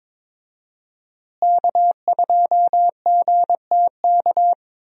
Morse code, heard as a keyed tone, about 22 wpm, 700 Hz. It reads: K2GTK
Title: USA Call Signs - 5 Characters - Repeated 3x - 22wpm